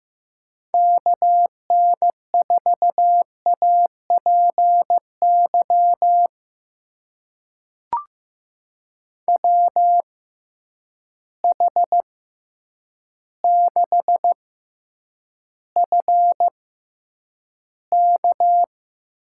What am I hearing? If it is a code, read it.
Morse code, heard as a keyed tone, about 15 wpm, spaced as for 8 wpm, 700 Hz. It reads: KN4APY WH6FK
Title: USA Call Signs - 6 Characters - 6x Character Spacing - 15wpm